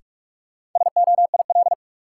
Morse code, heard as a keyed tone, about 41 wpm, 700 Hz. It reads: SOIL
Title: Sentences from Top 1000 Words - Review Part 4 - New Word Drill - 45wpm